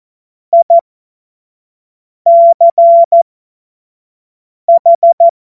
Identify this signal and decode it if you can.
Morse code, heard as a keyed tone, about 14 wpm, spaced as for 5 wpm, 680 Hz. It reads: ICH
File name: Morsekurs / Witze / Witz 16 April 2024 25 F70